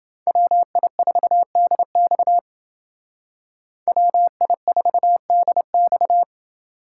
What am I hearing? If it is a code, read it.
Morse code, heard as a keyed tone, about 28 wpm, 700 Hz. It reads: WI4DX WI4DX